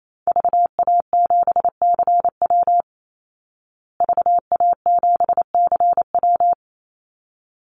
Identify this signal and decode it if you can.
Morse code, heard as a keyed tone, about 28 wpm, 700 Hz. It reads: VA7CW VA7CW
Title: Call Signs Repeated 3x - 28wpm